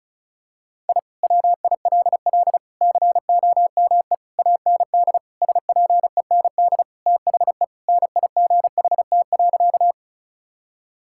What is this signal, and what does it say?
Morse code, heard as a keyed tone, about 36 wpm, 700 Hz. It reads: I WILL COME AND SPEND THE NIGHT.
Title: Sentences from Top 1000 Words - Review Part 5 - 35wpm